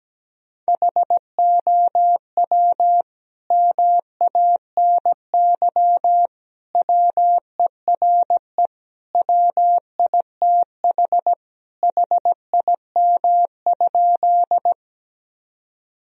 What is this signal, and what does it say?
Morse code, heard as a keyed tone, about 17 wpm, 700 Hz. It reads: HOW MANY WERE WITH HIM?
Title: Sentences from Top 1000 Words - Review Part 1 - 17wpm